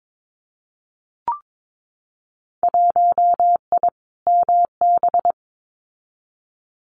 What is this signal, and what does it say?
Morse code, heard as a keyed tone, about 22 wpm, 700 Hz. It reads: 1I MB
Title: Winter Field Day Exchanges Encore 22wpm